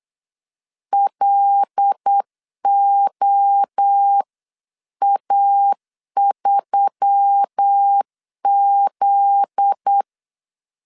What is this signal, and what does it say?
Morse code, heard as a keyed tone, about 8 wpm, 800 Hz. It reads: LOA3Z
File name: CW / DET / 1 9 1500C S C 8wpm 5653